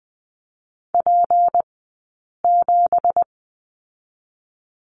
Morse code, heard as a keyed tone, about 19 wpm, 700 Hz. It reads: P 7
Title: 183 Lesson 30d - Letter b - Two Characters 20wpm